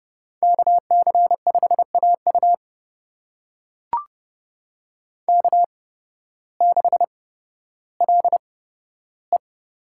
Morse code, heard as a keyed tone, about 30 wpm, spaced as for 14 wpm, 700 Hz. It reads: KC5AU K6LE
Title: USA Call Signs - 5 Characters - 8x Character Spacing - 30wpm